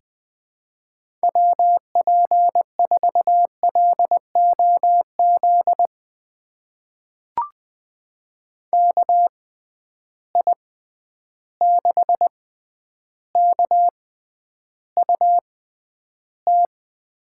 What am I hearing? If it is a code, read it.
Morse code, heard as a keyed tone, about 20 wpm, spaced as for 10 wpm, 700 Hz. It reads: WP4LOZ KI6KUT
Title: USA Call Signs - 6x Character Spacing - 20wpm